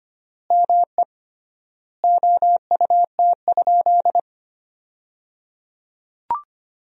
Morse code, heard as a keyed tone, about 25 wpm, 700 Hz. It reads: ME OUT?
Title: 3x Word-Spacing - Sentences from Top 200 Words - 25wpm